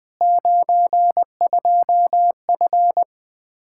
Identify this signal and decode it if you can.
Morse code, heard as a keyed tone, about 21 wpm, 700 Hz. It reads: 92F
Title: Practice QSO - Exchange 2 - Info - 20wpm